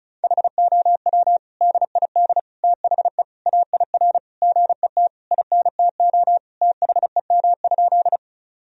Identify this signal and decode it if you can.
Morse code, heard as a keyed tone, about 36 wpm, 700 Hz. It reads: HOW DID THE AIR GET INTO THEM?
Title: Sentences from Top 1000 Words - Review Part 1 - 35wpm